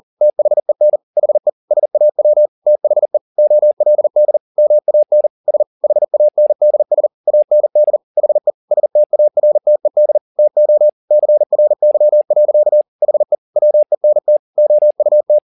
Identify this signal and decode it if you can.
Morse code, heard as a keyed tone, about 40 wpm, 600 Hz. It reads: THEN HE SAW THE OLD MAN S HANDS AND HE STARTED TO CRY. HE WENT OUT
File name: the old man and the sea 2 40wpm